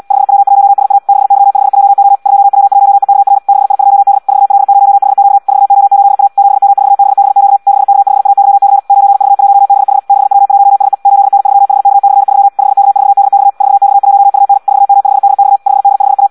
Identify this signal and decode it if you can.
Morse code, heard as a keyed tone, about 74 wpm, 800 Hz. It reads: 5W1SA 6Y5JO 2W1EKR 6I2D 4L0HQ 4O8A 7M4BBQ 6K5AYC 9V1BH 6W1SE 8N3HK/3 5K4DX 4X1UN 4M5DX 5D5A
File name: Audio practica CW / qrq-tx-78 2